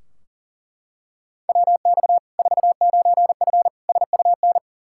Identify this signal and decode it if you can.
Morse code, heard as a keyed tone, about 40 wpm, 700 Hz. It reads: WX 49F SUN
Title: Practice QSO - Exchange 2 - Info - 40wpm